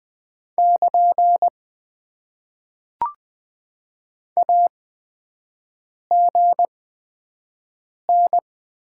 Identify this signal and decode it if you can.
Morse code, heard as a keyed tone, about 20 wpm, spaced as for 5 wpm, 700 Hz. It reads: <KN> AGN
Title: Top 90 QSO Elements - 8x Character Spacing - 20wpm